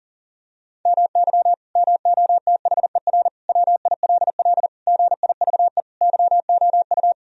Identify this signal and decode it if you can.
Morse code, heard as a keyed tone, about 40 wpm, 700 Hz. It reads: MY MOTHER WILL GIVE YOU
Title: Sentences from Top 1000 Words - Review Part 1 - 40wpm